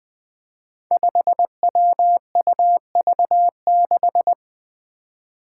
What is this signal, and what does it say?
Morse code, heard as a keyed tone, about 20 wpm, 700 Hz. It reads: 5WUV6
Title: Five Challenging Letter-Number Combinations 20wpm